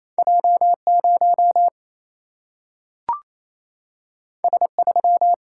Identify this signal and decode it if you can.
Morse code, heard as a keyed tone, about 28 wpm, 700 Hz. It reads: J 0 S 3
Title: Single-Letter Single-Number 28wpm